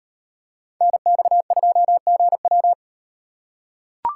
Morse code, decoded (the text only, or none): NX2GW